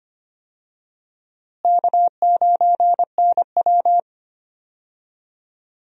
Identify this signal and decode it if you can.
Morse code, heard as a keyed tone, about 25 wpm, 700 Hz. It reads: K9NW
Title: USA Call Signs - 4 Characters - Repeated 2x - 25wpm